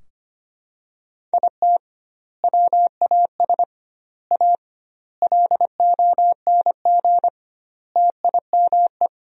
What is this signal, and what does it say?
Morse code, heard as a keyed tone, about 25 wpm, 700 Hz. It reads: IT WAS A LONG TIME
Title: 2x Word-Spacing - Sentences from Top 500 Words - 25wpm